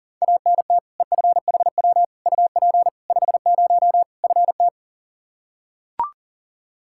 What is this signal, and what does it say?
Morse code, heard as a keyed tone, about 40 wpm, 700 Hz. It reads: ANT EFHW UP 50 FT
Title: Practice QSO - Exchange 2 - Info - 40wpm